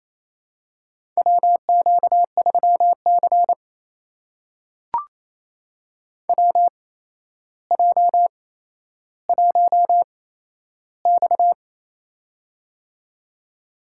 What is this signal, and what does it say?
Morse code, heard as a keyed tone, about 28 wpm, spaced as for 12 wpm, 700 Hz. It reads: WQ3C WJ1X
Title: USA Call Signs - 4 Characters - 8x Character Spacing - 28wpm